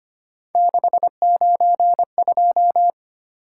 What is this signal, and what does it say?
Morse code, heard as a keyed tone, about 25 wpm, 700 Hz. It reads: 692
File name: Three Letter-Number Combinations - 25wpm